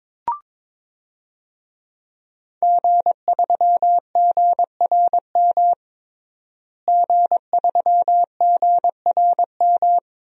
Morse code, heard as a keyed tone, about 22 wpm, 700 Hz. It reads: G3GRM G3GRM
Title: DXCC-100 Call Signs - 5 Characters - Repeated 3x - 22wpm